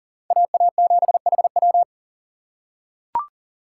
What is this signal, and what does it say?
Morse code, heard as a keyed tone, about 40 wpm, 700 Hz. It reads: AA7HW
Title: USA Call Signs - 5 Characters - 6x Character Spacing - 40wpm